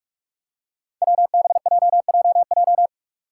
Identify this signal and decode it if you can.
Morse code, heard as a keyed tone, about 44 wpm, 700 Hz. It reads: WBJJJ
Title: Five Challenging Letter-Number Combinations 45wpm